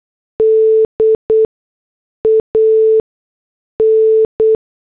Morse code, heard as a keyed tone, about 8 wpm, 440 Hz. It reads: D A N
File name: morse / headlines